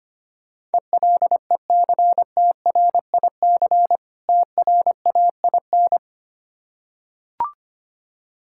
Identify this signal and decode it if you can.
Morse code, heard as a keyed tone, about 25 wpm, 700 Hz. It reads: ELECTRIC TRAIN
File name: practice / 25wpm / Difficult Words with Hints - 25wpm